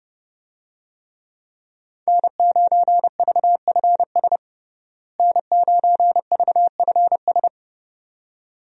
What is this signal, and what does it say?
Morse code, heard as a keyed tone, about 30 wpm, 700 Hz. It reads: N9VFS N9VFS
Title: USA Call Signs - 5 Characters - Repeated 2x - 30wpm